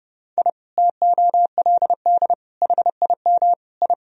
Morse code, heard as a keyed tone, about 30 wpm, 700 Hz. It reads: I TOLD HIM I
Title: Sentences from Top 1000 Words - Review Part 1 - 30wpm